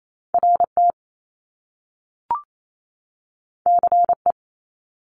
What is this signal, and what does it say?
Morse code, heard as a keyed tone, about 28 wpm, 700 Hz. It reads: RT CE
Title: Top Two Letter Combinations in Words 28wpm